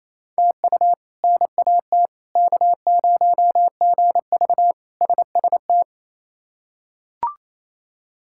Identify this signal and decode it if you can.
Morse code, heard as a keyed tone, about 27 wpm, 700 Hz. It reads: TU NAT K0GV SST
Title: K1USN SST QSO Practice - 28wpm